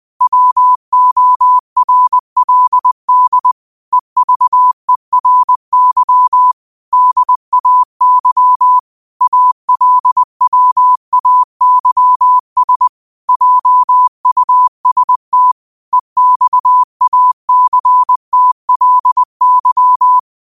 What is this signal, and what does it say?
Morse code, heard as a keyed tone, about 20 wpm, 1000 Hz. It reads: WORLD EVERY DAY ALWAYS JUST EXACTLY
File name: qotdfiles / 20wpm